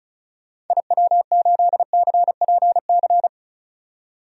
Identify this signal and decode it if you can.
Morse code, heard as a keyed tone, about 36 wpm, 700 Hz. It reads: IW8CPC
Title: DXCC-100 Call Signs - 6 Characters - 35wpm